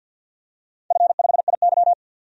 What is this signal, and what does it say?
Morse code, heard as a keyed tone, about 51 wpm, 700 Hz. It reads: F5IY
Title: DXCC-100 Call Signs - 4 Characters - 50wpm